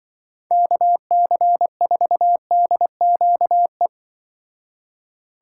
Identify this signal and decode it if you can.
Morse code, heard as a keyed tone, about 24 wpm, 700 Hz. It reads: KC4DQE
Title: Speed Racing - USA 2x3 Call Signs 20wpm